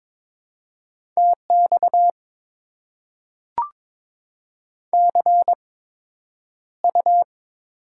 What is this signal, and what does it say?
Morse code, heard as a keyed tone, about 22 wpm, spaced as for 9 wpm, 700 Hz. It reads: TX CU
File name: Top 90 QSO Elements - 8x Character Spacing - 22wpm